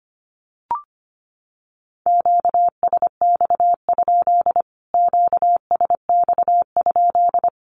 Morse code, heard as a keyed tone, about 25 wpm, 700 Hz. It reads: QSX? QSX?
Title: Q-Codes Repeated 3x 25wpm